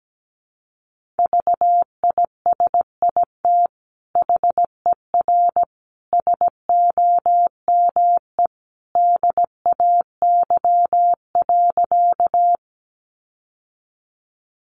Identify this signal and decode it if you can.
Morse code, heard as a keyed tone, about 17 wpm, 700 Hz. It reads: VISIT HER SOME DAY.
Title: Sentences from Top 900 Words 17wpm